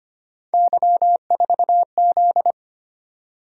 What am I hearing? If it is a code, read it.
Morse code, heard as a keyed tone, about 25 wpm, 700 Hz. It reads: Y4Z
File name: Three Letter-Number Combinations - 25wpm